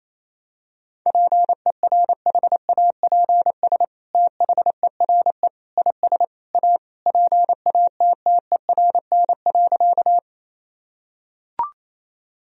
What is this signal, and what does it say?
Morse code, heard as a keyed tone, about 27 wpm, 700 Hz. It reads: PERHAPS THERE IS A PATTERN.
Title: Sentences from Top 1000 Words - Review Part 2 - 28wpm